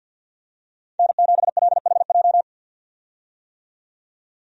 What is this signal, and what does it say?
Morse code, heard as a keyed tone, about 52 wpm, 700 Hz. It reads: N7LHJ
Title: USA Call Signs - 5 Characters - 50wpm